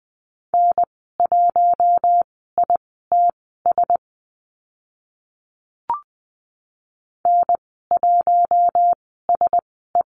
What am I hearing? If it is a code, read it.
Morse code, heard as a keyed tone, about 20 wpm, spaced as for 14 wpm, 700 Hz. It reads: N1ITS N1SE
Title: 042 Lesson 8g - Number 1 - Call Signs 20wpm